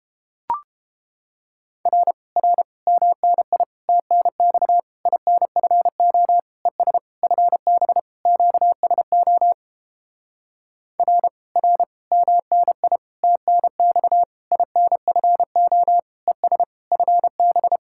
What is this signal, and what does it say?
Morse code, heard as a keyed tone, about 29 wpm, 700 Hz. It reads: R R MNI TNX INFO ES FB QSO R R MNI TNX INFO ES FB